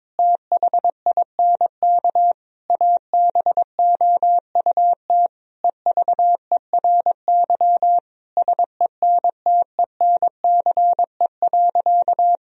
Morse code, heard as a keyed tone, about 22 wpm, 700 Hz. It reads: THINK ABOUT EVERY SENTENCE.
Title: Sentences from Top 1000 Words - Review Part 1 - 22wpm